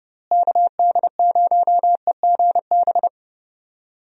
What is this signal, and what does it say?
Morse code, heard as a keyed tone, about 30 wpm, 700 Hz. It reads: KD0EGB
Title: USA Call Signs - 6 Characters - Repeated 2x - 30wpm